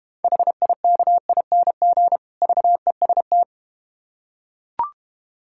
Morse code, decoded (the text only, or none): HIKING VEST